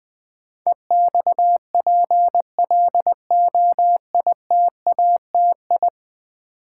Morse code, heard as a keyed tone, about 20 wpm, 700 Hz. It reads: EXPLOITATI
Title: Build A Word - 5 Letters At A Time - 12-Letter Words - 20wpm